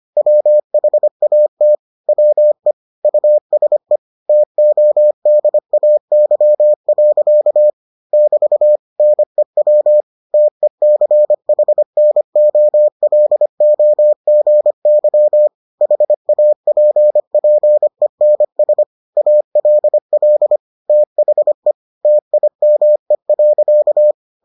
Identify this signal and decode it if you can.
Morse code, heard as a keyed tone, about 25 wpm, 600 Hz. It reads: WHAT WE USE TODAY. <BT> NEW TECHNOLOGY HAPPENS ALL THE TIME.